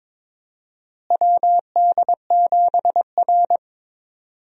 Morse code, heard as a keyed tone, about 22 wpm, 700 Hz. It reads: WD7R
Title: USA Call Signs - 4 Characters - Repeated 3x - 22wpm